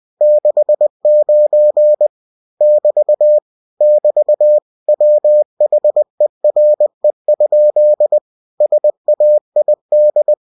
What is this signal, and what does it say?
Morse code, heard as a keyed tone, about 20 wpm, 600 Hz. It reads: 69 <BT> <BT> WHERE? SAID
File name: cw-books / winnie-the-pooh / winnie-20wpm-page-0069